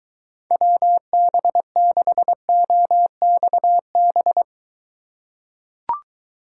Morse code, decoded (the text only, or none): WB6OXB